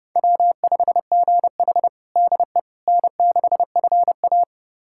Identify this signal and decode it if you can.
Morse code, heard as a keyed tone, about 30 wpm, 700 Hz. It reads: W5GH DE N6FA
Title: USA Callsign DE Callsign - 4 Characters - 30wpm